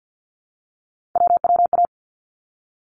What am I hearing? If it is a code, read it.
Morse code, heard as a keyed tone, about 50 wpm, 700 Hz. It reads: WWA
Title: ARRL Sections 50wpm